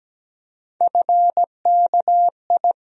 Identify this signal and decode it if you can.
Morse code, heard as a keyed tone, about 17 wpm, 700 Hz. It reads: FKI
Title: USA Call Sign Postfixes - 3 Letters 17wpm